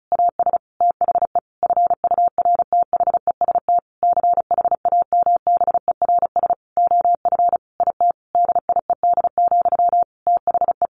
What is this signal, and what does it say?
Morse code, heard as a keyed tone, about 34 wpm, 700 Hz. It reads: AS THE FURTHEST CHAMBERS OF IT DIED, THE
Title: A Sound Of Thunder 35wpm